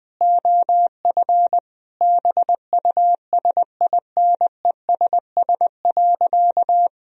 Morse code, heard as a keyed tone, about 21 wpm, 700 Hz. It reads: OF BUSINESS.
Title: Sentences from Top 1000 Words - Review Part 3 - 20wpm